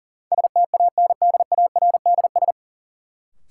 Morse code, heard as a keyed tone, about 40 wpm, 700 Hz. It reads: STANDARDS
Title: Sentences from Top 1000 Words - Review Part 6 - New Word Drill - 40wpm